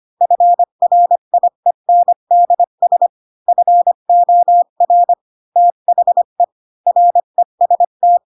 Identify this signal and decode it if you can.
Morse code, heard as a keyed tone, about 25 wpm, 700 Hz. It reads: FRIENDS FOR THE REST